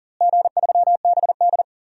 Morse code, heard as a keyed tone, about 37 wpm, 700 Hz. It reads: G3BD
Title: DXCC-100 Call Signs - 4 Characters - Repeated 2x - 40wpm